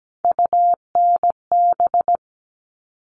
Morse code, heard as a keyed tone, about 17 wpm, 700 Hz. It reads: UNB
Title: All Cut Numbers - 3-Digit 17wpm